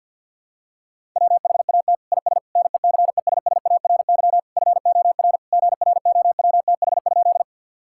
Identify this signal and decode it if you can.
Morse code, heard as a keyed tone, about 50 wpm, 700 Hz. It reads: WHAT IS NECESSARY FOR GROWTH?
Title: Sentences from Top 1000 Words - Review Part 5 - 50wpm